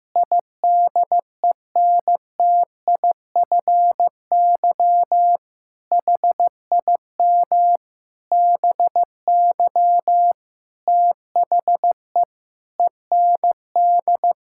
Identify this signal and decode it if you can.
Morse code, heard as a keyed tone, about 15 wpm, 700 Hz. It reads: IDENTIFY HIM BY THE END
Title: Sentences from Top 1000 Words - Review Part 6 - 15wpm